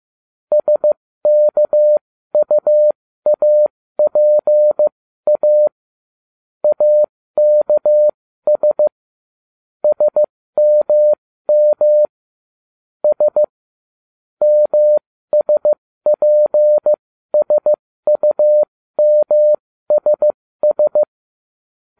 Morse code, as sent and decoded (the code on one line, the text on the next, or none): ... -.- ..- .- .--. .- / .- -.- ... / ... -- -- / ... / -- ... .--. ... ..- -- ... ...
SKUAPA AKS SMM S MSPSUMSS